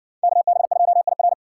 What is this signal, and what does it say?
Morse code, heard as a keyed tone, about 60 wpm, 700 Hz. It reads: KB2IR